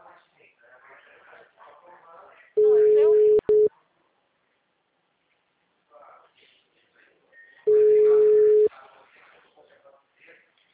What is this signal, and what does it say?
Ringback tone (Europe, 425 Hz), with speech.